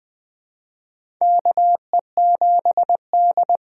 Morse code, heard as a keyed tone, about 20 wpm, 700 Hz. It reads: KE7D
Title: USA Call Signs - 6 Characters - 6x Character Spacing - 20wpm